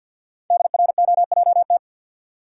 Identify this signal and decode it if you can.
Morse code, heard as a keyed tone, about 50 wpm, 700 Hz. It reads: DROJT
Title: Code Groups - 5 Letters - 50wpm